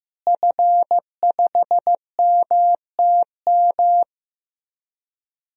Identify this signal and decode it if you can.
Morse code, heard as a keyed tone, about 15 wpm, 700 Hz. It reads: F5MTM
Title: DXCC-100 Call Signs - 5 Characters - Repeated 3x - 15wpm